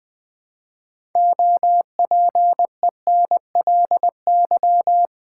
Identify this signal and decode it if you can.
Morse code, heard as a keyed tone, about 20 wpm, 700 Hz. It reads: OPENLY
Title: 6 Letter Words - First 3 Letters Given - 20wpm